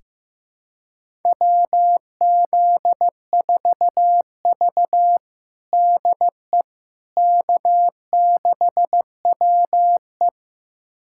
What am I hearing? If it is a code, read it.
Morse code, heard as a keyed tone, about 15 wpm, 700 Hz. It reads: WZ4V DE K6WE
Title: USA Callsign DE Callsign - 4 Characters - 15wpm